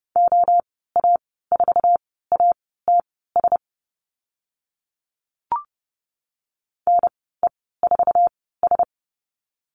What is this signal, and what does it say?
Morse code, heard as a keyed tone, about 30 wpm, spaced as for 17 wpm, 690 Hz. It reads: OA4ATS NE4S
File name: Lesson 15g - Number 5 - Call Signs 30wpm